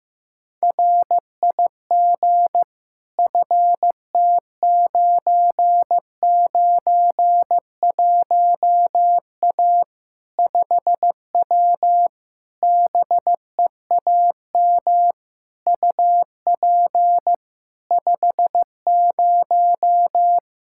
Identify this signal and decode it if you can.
Morse code, heard as a keyed tone, about 15 wpm, 700 Hz. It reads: RIG FT991A 5W BEAM UP 50
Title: Practice QSO - Exchange 2 - Info - 15wpm